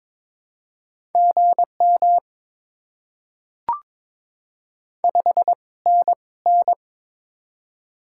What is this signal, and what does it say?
Morse code, heard as a keyed tone, about 22 wpm, spaced as for 18 wpm, 690 Hz. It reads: GM 5NN
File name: Top 20 QSO Elements - 2x Character Spacing - 22wpm